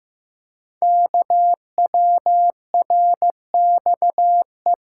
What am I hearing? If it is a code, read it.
Morse code, heard as a keyed tone, about 15 wpm, 700 Hz. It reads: KWRXE
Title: Code Groups - 5 Letters - 15wpm